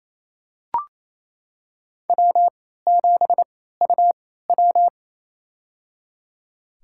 Morse code, heard as a keyed tone, about 28 wpm, spaced as for 16 wpm, 700 Hz. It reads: W7UW
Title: USA Call Signs - 4 Characters - 3x Character Spacing - 28wpm